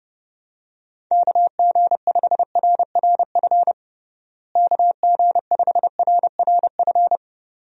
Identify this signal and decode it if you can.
Morse code, heard as a keyed tone, about 32 wpm, 700 Hz. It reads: KG5RRF KG5RRF
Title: USA Call Signs - 6 Characters - Repeated 2x - 30wpm